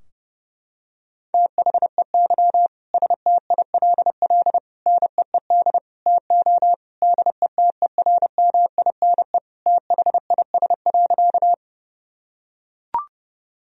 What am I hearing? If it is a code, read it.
Morse code, heard as a keyed tone, about 30 wpm, 700 Hz. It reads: THEY STILL NEED TO DETERMINE THIS.
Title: Sentences from Top 1000 Words - Review Part 4 - 30wpm